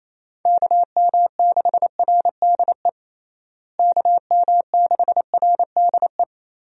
Morse code, heard as a keyed tone, about 28 wpm, 700 Hz. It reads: KM6RDE KM6RDE
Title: USA Call Signs - 6 Characters - Repeated 2x - 28wpm